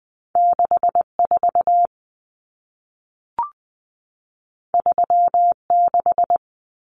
Morse code, heard as a keyed tone, about 20 wpm, 700 Hz. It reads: 64 36
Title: Common Cut Numbers - 2-Digit 20wpm